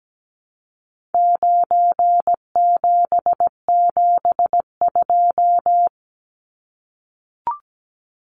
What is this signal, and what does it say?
Morse code, heard as a keyed tone, about 17 wpm, 700 Hz. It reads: 9772
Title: Numbers - 4-Digit 17wpm